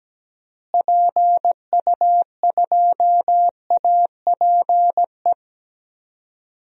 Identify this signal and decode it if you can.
Morse code, heard as a keyed tone, about 17 wpm, 700 Hz. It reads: PU2APE
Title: DXCC-100 Call Signs - 6 Characters - Repeated 3x - 17wpm